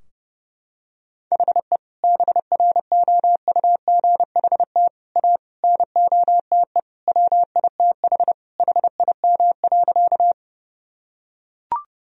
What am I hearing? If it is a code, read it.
Morse code, heard as a keyed tone, about 32 wpm, 700 Hz. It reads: HE BROUGHT A NOTE WITH HIM.
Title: Sentences from Top 1000 Words - Review Part 2 - 30wpm